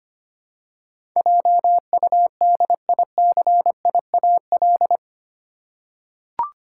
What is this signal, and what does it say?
Morse code, heard as a keyed tone, about 25 wpm, 700 Hz. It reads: JUDICIAL
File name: Eight Letter Words - 25wpm